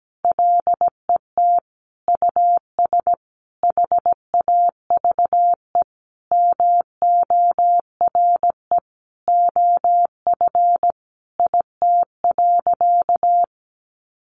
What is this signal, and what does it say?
Morse code, heard as a keyed tone, about 17 wpm, 700 Hz. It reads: LET US HAVE MORE OF IT.
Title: Sentences from Top 1300 Words 17wpm